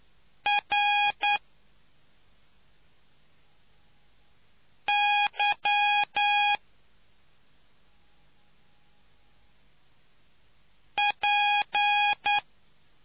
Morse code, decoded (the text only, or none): RY P